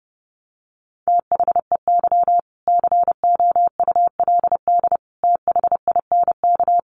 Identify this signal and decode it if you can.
Morse code, heard as a keyed tone, about 30 wpm, 700 Hz. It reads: THEY COULD THINK
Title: Sentences from Top 800 Words 30wpm